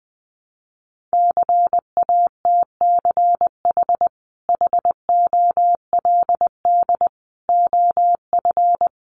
Morse code, heard as a keyed tone, about 20 wpm, 700 Hz. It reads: CATCH HOLD OF
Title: Sentences from Top 1500 Words 20wpm